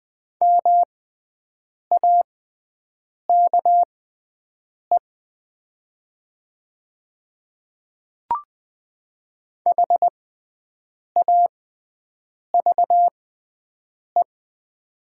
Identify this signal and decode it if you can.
Morse code, heard as a keyed tone, about 20 wpm, spaced as for 7 wpm, 700 Hz. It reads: MAKE HAVE
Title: 6x Character Spacing - Top 200 Words - 20wpm